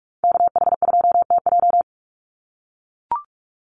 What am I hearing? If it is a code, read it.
Morse code, heard as a keyed tone, about 44 wpm, 700 Hz. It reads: KH2TJ
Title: NA SOTA Activator Callsigns 45wpm